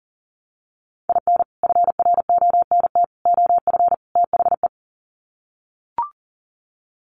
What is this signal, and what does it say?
Morse code, heard as a keyed tone, about 40 wpm, 700 Hz. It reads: IN FRONT OF THE
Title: Sets of 4 Words 40wpm